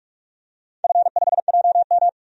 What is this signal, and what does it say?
Morse code, heard as a keyed tone, about 46 wpm, 700 Hz. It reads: F5JM
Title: DXCC-100 Call Signs - 4 Characters - 45wpm